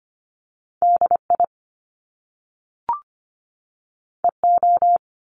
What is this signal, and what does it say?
Morse code, heard as a keyed tone, about 25 wpm, 700 Hz. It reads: D I E O
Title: Top Two Letter Combinations in Words 25wpm